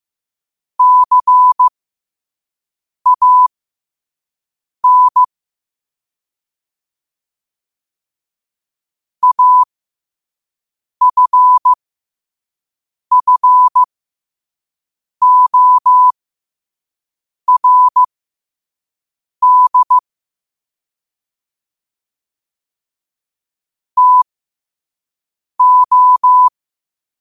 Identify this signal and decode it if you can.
Morse code, heard as a keyed tone, about 15 wpm, spaced as for 5 wpm, 1000 Hz. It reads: CAN AFFORD TO
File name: qotdfiles / 5wpm